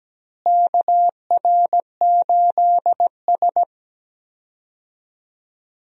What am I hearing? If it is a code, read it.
Morse code, heard as a keyed tone, about 17 wpm, 700 Hz. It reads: KR8S
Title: USA Call Signs - 4 Characters - Repeated 3x - 17wpm